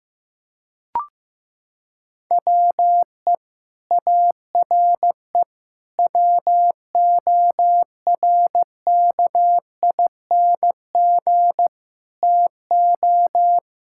Morse code, heard as a keyed tone, about 15 wpm, 700 Hz. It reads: WE ARE WORKING TO
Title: Sentences from Top 1000 Words - Review Part 6 - 15wpm